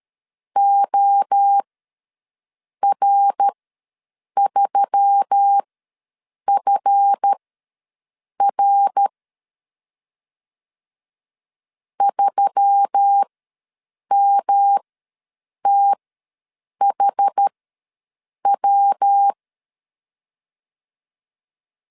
Morse code, heard as a keyed tone, about 13 wpm, spaced as for 7 wpm, 800 Hz. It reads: OR3FR 3MTHW
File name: LEZIONI / 7 GF9Z / 7 GF9 ws 12wpm 3344